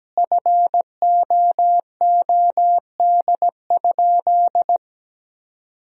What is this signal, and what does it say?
Morse code, heard as a keyed tone, about 17 wpm, 700 Hz. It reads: FOOD?
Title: 2x Word-Spacing - Sentences from Top 300 Words - 17wpm